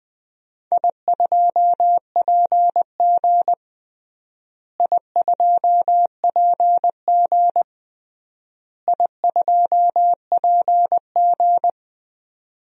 Morse code, heard as a keyed tone, about 20 wpm, 700 Hz. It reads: I2PG I2PG I2PG
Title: DXCC-100 Call Signs - 4 Characters - Repeated 3x - 20wpm